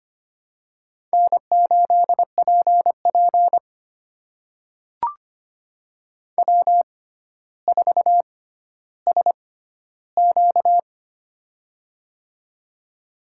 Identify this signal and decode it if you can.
Morse code, heard as a keyed tone, about 25 wpm, spaced as for 13 wpm, 700 Hz. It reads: N8PP W4SQ
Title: USA Call Signs - 4 Characters - 6x Character Spacing - 25wpm